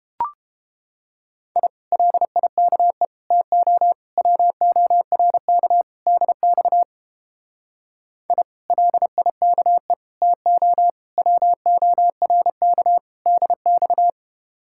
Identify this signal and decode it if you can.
Morse code, heard as a keyed tone, about 29 wpm, 700 Hz. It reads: I LIKE TO WORK DX I LIKE TO WORK DX